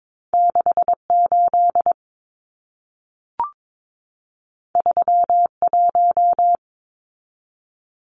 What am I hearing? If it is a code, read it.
Morse code, heard as a keyed tone, about 22 wpm, 700 Hz. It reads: 68 31
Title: Numbers - 2-Digit 22wpm